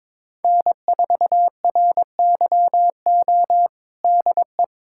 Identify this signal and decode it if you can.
Morse code, heard as a keyed tone, about 22 wpm, 700 Hz. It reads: N4RYO DE
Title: USA Callsign DE Callsign - 5 Characters - 22wpm